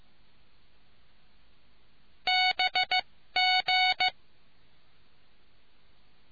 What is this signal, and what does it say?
Morse code, heard as a keyed tone, about 15 wpm, spaced as for 13 wpm, 740 Hz. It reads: BG